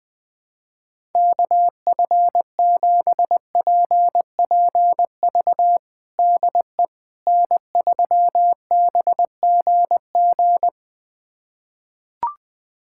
Morse code, heard as a keyed tone, about 20 wpm, 700 Hz. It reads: KF7PPV DE N3BGG
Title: USA Callsign DE Callsign 20wpm